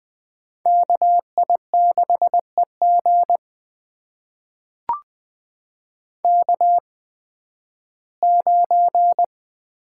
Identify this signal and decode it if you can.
Morse code, heard as a keyed tone, about 20 wpm, spaced as for 13 wpm, 700 Hz. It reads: KI6EG K9
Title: USA Call Signs - 8x Character Spacing - 20wpm